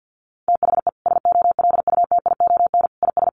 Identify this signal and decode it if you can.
Morse code, heard as a keyed tone, about 48 wpm, 700 Hz. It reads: THE SOLUTION IS
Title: Sentences from Top 1300 Words 50wpm